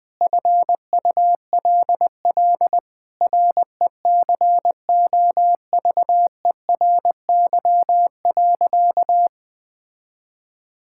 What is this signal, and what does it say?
Morse code, heard as a keyed tone, about 20 wpm, 700 Hz. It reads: FULL RECOVERY.
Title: Sentences with 7-Word Phrases - Spoken Beginning - 20wpm